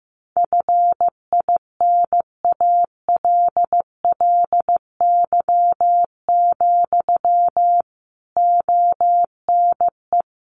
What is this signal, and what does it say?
Morse code, heard as a keyed tone, about 15 wpm, 700 Hz. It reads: FINALLY, ONE
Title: Sentences from Top 1300 Words 15wpm